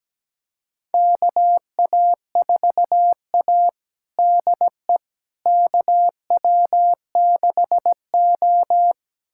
Morse code, heard as a keyed tone, about 17 wpm, 700 Hz. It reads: KA4A DE KW6O
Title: USA Callsign DE Callsign - 4 Characters - 17wpm